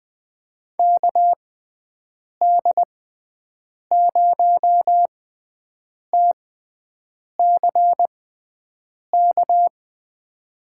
Morse code, heard as a keyed tone, about 20 wpm, spaced as for 7 wpm, 700 Hz. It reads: KD0TCK
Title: USA Call Signs - 6 Characters - 6x Character Spacing - 20wpm